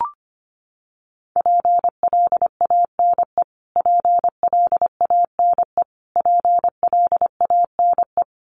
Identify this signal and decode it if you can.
Morse code, heard as a keyed tone, about 25 wpm, 700 Hz. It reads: PLANE PLANE PLANE
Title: Top 500 Words Repeated 25wpm